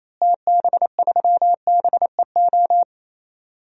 Morse code, heard as a keyed tone, about 27 wpm, 700 Hz. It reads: TB3BEO
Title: DXCC-100 Call Signs - 6 Characters - Repeated 2x - 28wpm